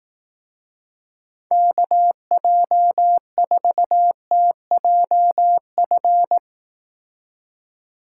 Morse code, heard as a keyed tone, about 19 wpm, 700 Hz. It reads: KJ4TJF
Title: Speed Racing - USA Call Signs 15wpm